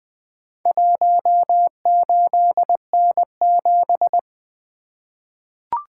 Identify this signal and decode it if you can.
Morse code, heard as a keyed tone, about 20 wpm, 700 Hz. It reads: 18N7
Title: Four Letter-Number Combinations - 20wpm